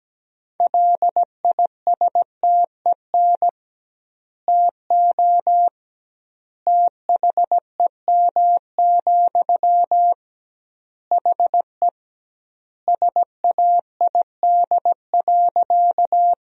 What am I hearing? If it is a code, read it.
Morse code, heard as a keyed tone, about 17 wpm, 700 Hz. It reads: LISTEN TO THEM, HE SAID.
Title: 2x Word-Spacing - Sentences from Top 400 Words - 17wpm